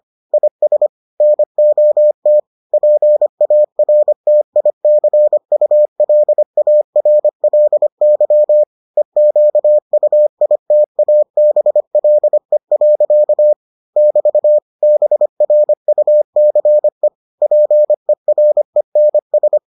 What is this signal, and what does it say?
Morse code, heard as a keyed tone, about 25 wpm, 600 Hz. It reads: IS NOT PARTICULARLY EQUITABLE. <BT> BRUCE PERENS